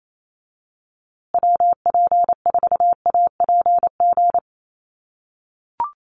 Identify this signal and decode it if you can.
Morse code, heard as a keyed tone, about 28 wpm, 700 Hz. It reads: WP4APG
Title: USA Call Signs Repeated 3x 28wpm